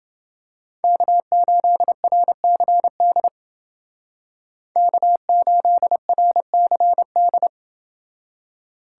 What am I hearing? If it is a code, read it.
Morse code, heard as a keyed tone, about 28 wpm, 700 Hz. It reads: K8RCD K8RCD